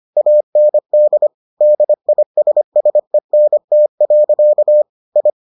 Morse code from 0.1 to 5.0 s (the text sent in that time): AND DISSENT.